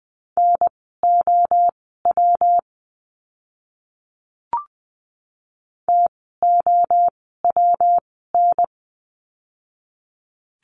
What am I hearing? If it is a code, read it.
Morse code, heard as a keyed tone, about 20 wpm, spaced as for 14 wpm, 700 Hz. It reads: NOW TOWN
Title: 115 Lesson 19f - Letter w - Words 20wpm